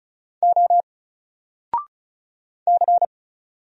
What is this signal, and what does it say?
Morse code, heard as a keyed tone, about 36 wpm, 700 Hz. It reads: O C
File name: Single-Letter-Number-35wpm